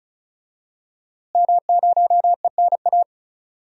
Morse code, heard as a keyed tone, about 34 wpm, 700 Hz. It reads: M0ENA
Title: DXCC-100 Call Signs - 5 Characters - Repeated 3x - 35wpm